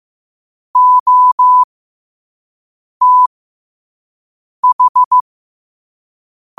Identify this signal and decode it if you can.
Morse code, heard as a keyed tone, about 15 wpm, spaced as for 5 wpm, 1000 Hz. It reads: OTH
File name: qotdfiles / 5wpm